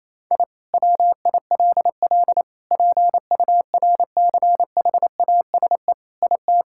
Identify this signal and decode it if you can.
Morse code, heard as a keyed tone, about 27 wpm, 700 Hz. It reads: I WILL PURCHASE IT
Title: Sentences from Top 1000 Words - Review Part 5 - 28wpm